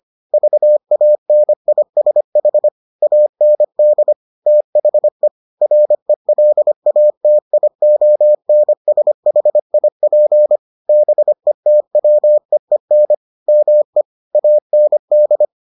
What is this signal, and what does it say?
Morse code, heard as a keyed tone, about 25 wpm, 600 Hz. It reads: VANISH AND THE RELATIONSHIP BETWEEN ME AND